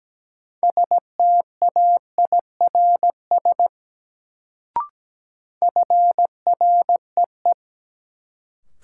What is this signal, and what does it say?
Morse code, heard as a keyed tone, about 17 wpm, 700 Hz. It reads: STAIRS FREE
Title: Sets of 5 Words Encore - New Word Drill - 17wpm